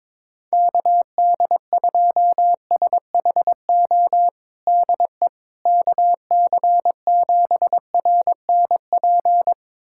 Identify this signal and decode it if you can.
Morse code, heard as a keyed tone, about 22 wpm, 700 Hz. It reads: KD2SHO DE KC7RNP
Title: USA Callsign DE Callsign - 6 Characters - 22wpm